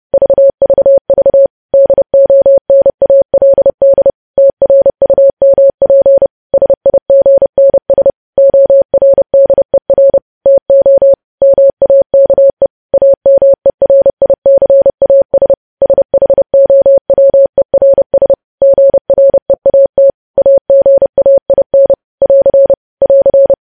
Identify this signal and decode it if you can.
Morse code, heard as a keyed tone, about 30 wpm, 550 Hz. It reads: VVV DONALD TRUMP SIGNS ORDER TO MAKE AMERICAS SHOWERS GREAT AGAIN <AR> <AR>
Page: news headlines in morse code at 30 wpm